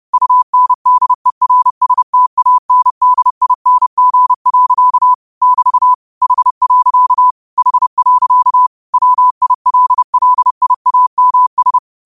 Morse code, heard as a keyed tone, about 30 wpm, 1000 Hz. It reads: UNDERSTANDING. <BT> H. H. WILLIAMS
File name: qotdfiles / 30wpm